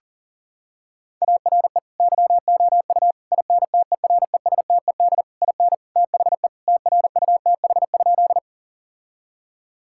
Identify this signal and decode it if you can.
Morse code, heard as a keyed tone, about 40 wpm, 700 Hz. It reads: ARE YOU INTERESTED IN THE TRUTH?
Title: Sentences from Top 1000 Words - Review Part 6 - 40wpm